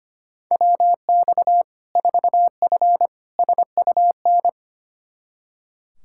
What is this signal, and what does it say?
Morse code, heard as a keyed tone, about 25 wpm, 700 Hz. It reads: WX 4F SUN
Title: Practice QSO - Exchange 2 - Info - 25wpm